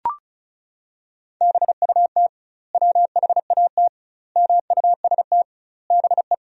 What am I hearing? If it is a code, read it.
Morse code, heard as a keyed tone, about 36 wpm, 700 Hz. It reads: BUT WHAT MUST BE
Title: 2x Word-Spacing - Sentences from Top 500 Words - 35wpm